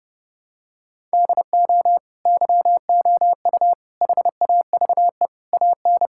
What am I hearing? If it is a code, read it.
Morse code, heard as a keyed tone, about 32 wpm, 700 Hz. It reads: DO YOU HAVE AN
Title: Sentences from Top 1000 Words - Review Part 5 - 30wpm